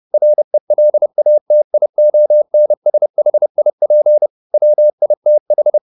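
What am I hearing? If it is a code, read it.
Morse code, heard as a keyed tone, about 29 wpm, 600 Hz. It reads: RELATIONSHIP WITH
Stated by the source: Morse code at 30 wpm, the novel Never by Ken Follett in English, from CW med Gnister.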